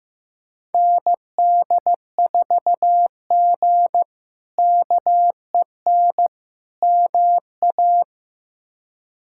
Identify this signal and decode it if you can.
Morse code, heard as a keyed tone, about 15 wpm, 700 Hz. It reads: ND4G KEN MA
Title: K1USN SST QSO Practice - 15wpm